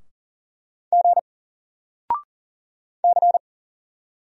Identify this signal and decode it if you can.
Morse code, heard as a keyed tone, about 40 wpm, 700 Hz. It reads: G C
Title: Single Letter-Number Alt 40wpm